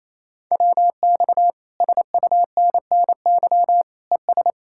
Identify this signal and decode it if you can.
Morse code, heard as a keyed tone, about 37 wpm, 700 Hz. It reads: WX SUNNY ES